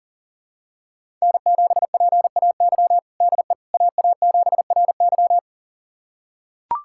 Morse code, heard as a keyed tone, about 37 wpm, 700 Hz. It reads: N7PAY DE AA7RY
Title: USA Callsign DE Callsign - 5 Characters - 40wpm